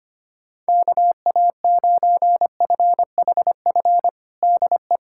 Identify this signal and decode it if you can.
Morse code, heard as a keyed tone, about 25 wpm, 700 Hz. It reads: KA9FHF DE
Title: USA Callsign DE Callsign - 6 Characters - 25wpm